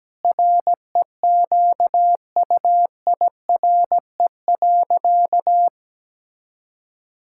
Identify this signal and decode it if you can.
Morse code, heard as a keyed tone, about 17 wpm, 700 Hz. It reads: REQUIRE.
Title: Sentences from Top 1000 Words - Review Part 6 - 17wpm